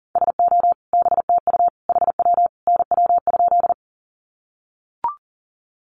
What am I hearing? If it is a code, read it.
Morse code, heard as a keyed tone, about 40 wpm, 700 Hz. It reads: SO BTU HW NW?
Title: Common QSO Phrases 40wpm